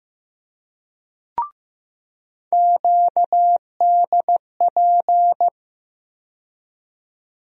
Morse code, heard as a keyed tone, about 15 wpm, 700 Hz. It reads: QDP
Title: USA Call Sign Postfixes - 3 Letters 15wpm